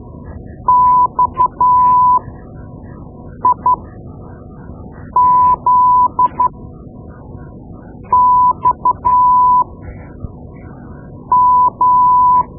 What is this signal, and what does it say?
Morse code, heard as a keyed tone, about 12 wpm, spaced as for 5 wpm, 1000 Hz. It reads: XIZXM
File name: Morse Lessons / Morse Lessons / Tape-2B